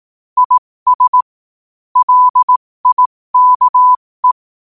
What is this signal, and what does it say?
Morse code, heard as a keyed tone, about 18 wpm, spaced as for 15 wpm, 1000 Hz. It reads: IS LIKE
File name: qotdfiles / 15wpm